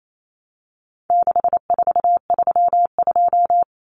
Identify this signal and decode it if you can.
Morse code, heard as a keyed tone, about 28 wpm, 700 Hz. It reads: 6432
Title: Common Cut Numbers - 4-Digit 28wpm